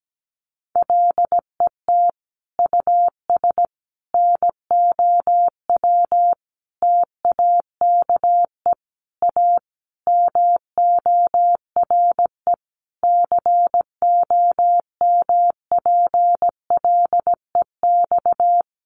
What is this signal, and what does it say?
Morse code, heard as a keyed tone, about 17 wpm, 700 Hz. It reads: LET US NOW TAKE A MORE COMPLEX
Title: Sentences from Top 900 Words 17wpm